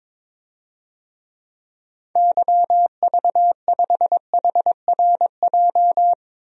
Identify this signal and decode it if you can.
Morse code, heard as a keyed tone, about 22 wpm, 700 Hz. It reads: YV5HRJ
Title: DXCC-100 Call Signs - 6 Characters - 22wpm